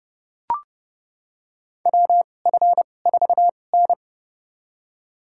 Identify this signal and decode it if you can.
Morse code, heard as a keyed tone, about 31 wpm, spaced as for 22 wpm, 700 Hz. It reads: WF4N
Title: USA Call Signs - 4 Characters - 2x Character Spacing - 30wpm